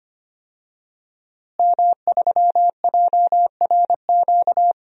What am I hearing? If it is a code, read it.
Morse code, heard as a keyed tone, about 25 wpm, 700 Hz. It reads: M3JRQ
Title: DXCC-100 Call Signs - 5 Characters - Repeated 2x - 25wpm